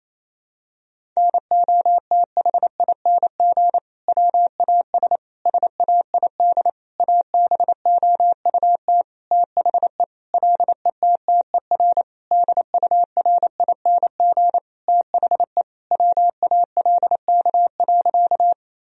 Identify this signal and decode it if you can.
Morse code, heard as a keyed tone, about 28 wpm, 700 Hz. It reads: NOTHING WAS SAID ABOUT THE LETTER DURING THE WALK.
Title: Sentences from Top 1000 Words - Review Part 2 - 28wpm